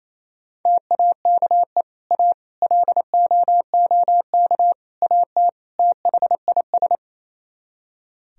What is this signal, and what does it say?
Morse code, heard as a keyed tone, about 38 wpm, 700 Hz. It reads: TAKE A LOOK AT THIS